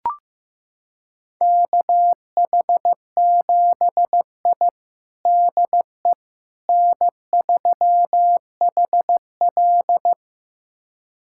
Morse code, heard as a keyed tone, about 16 wpm, 700 Hz. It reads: KH7I DE N3HL
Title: USA Callsign DE Callsign - 4 Characters - 15wpm